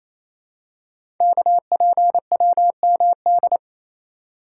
Morse code, heard as a keyed tone, about 29 wpm, 700 Hz. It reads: KPWMD
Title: Code Groups - 5 Letters - 28wpm